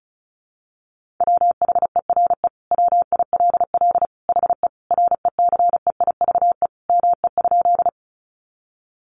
Morse code, heard as a keyed tone, about 34 wpm, 700 Hz. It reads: WHERE WILL HE RECEIVE ME?
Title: Sentences from Top 1000 Words 35wpm